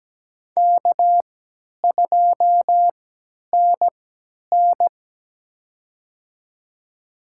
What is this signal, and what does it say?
Morse code, heard as a keyed tone, about 17 wpm, spaced as for 10 wpm, 700 Hz. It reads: K2NN
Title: USA Call Signs - 3x Character Spacing - 17wpm